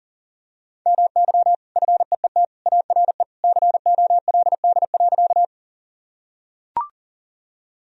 Morse code, heard as a keyed tone, about 40 wpm, 700 Hz. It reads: MY FEET ARE COLD.
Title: Sentences from Top 1000 Words - Review Part 1 - 40wpm